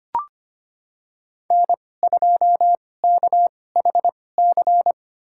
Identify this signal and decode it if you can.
Morse code, heard as a keyed tone, about 25 wpm, spaced as for 18 wpm, 700 Hz. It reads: N2KHC
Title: USA Call Signs - 2x Character Spacing - 25wpm